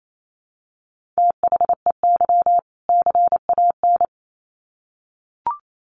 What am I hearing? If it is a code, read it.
Morse code, heard as a keyed tone, about 28 wpm, 700 Hz. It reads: THEY CAN
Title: Sets of 2 Words 28wpm